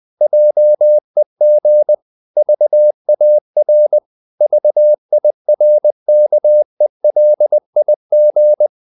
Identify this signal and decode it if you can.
Morse code, heard as a keyed tone, about 20 wpm, 600 Hz. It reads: JEG VAR VIRKELIG
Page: Ferie 20 wpm